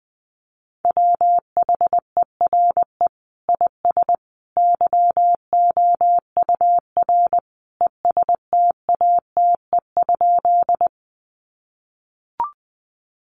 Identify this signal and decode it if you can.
Morse code, heard as a keyed tone, about 20 wpm, 700 Hz. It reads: WHERE IS YOUR ESTATE?
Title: Sentences from Top 1200 Words 20wpm